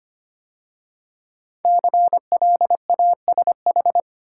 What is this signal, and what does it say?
Morse code, heard as a keyed tone, about 25 wpm, 700 Hz. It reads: CLASH
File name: 5 Letter Words - First 3 Letters Given - 25wpm